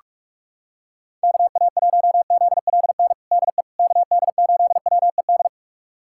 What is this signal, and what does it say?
Morse code, heard as a keyed tone, about 46 wpm, 700 Hz. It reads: KA1ZLN DE KD8WED
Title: USA Callsign DE Callsign - 6 Characters - 45wpm